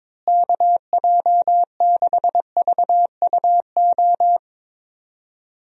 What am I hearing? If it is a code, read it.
Morse code, heard as a keyed tone, about 22 wpm, 700 Hz. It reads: KJ6VUO
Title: USA Call Signs - 6 Characters - Repeated 3x - 22wpm